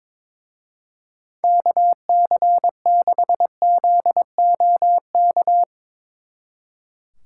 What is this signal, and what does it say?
Morse code, heard as a keyed tone, about 22 wpm, 700 Hz. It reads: KC6ZOK
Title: Speed Racing - USA 2x3 Call Signs 22wpm